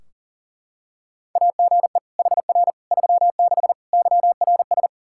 Morse code, heard as a keyed tone, about 37 wpm, 700 Hz. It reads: AGE HR 36 YRS
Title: Practice QSO - Exchange 3 - More Info - 40wpm